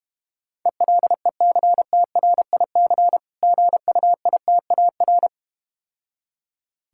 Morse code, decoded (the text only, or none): ELECTRIC GUITAR